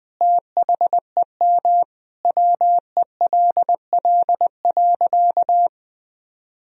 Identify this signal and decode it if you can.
Morse code, heard as a keyed tone, about 20 wpm, 700 Hz. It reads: THEM WELL.
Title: Sentences from Top 1000 Words - Review Part 1 - 20wpm